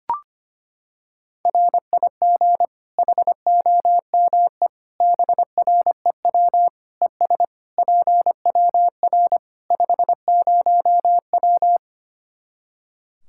Morse code, heard as a keyed tone, about 25 wpm, 700 Hz. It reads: RIG HOME BREW ES PWR 50W
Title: Practice QSO - Exchange 2 - Info - 25wpm